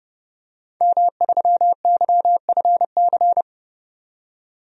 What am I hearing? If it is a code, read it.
Morse code, heard as a keyed tone, about 30 wpm, 700 Hz. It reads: M3YFC
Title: DXCC-100 Call Signs - 5 Characters - 30wpm